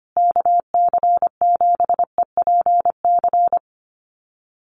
Morse code, heard as a keyed tone, about 25 wpm, 700 Hz. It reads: KC7EPC
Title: USA Call Signs Repeated 3x 25wpm